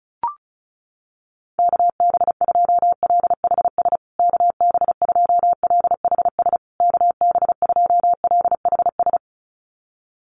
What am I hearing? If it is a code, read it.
Morse code, heard as a keyed tone, about 34 wpm, 700 Hz. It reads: KB2LHS KB2LHS KB2LHS
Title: USA Call Signs Repeated 3x 35wpm